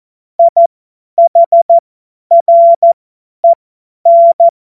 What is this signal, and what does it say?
Morse code, heard as a keyed tone, about 14 wpm, spaced as for 10 wpm, 680 Hz. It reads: IHREN
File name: Morsekurs / Witze / Witz 30. August 2022 50 F70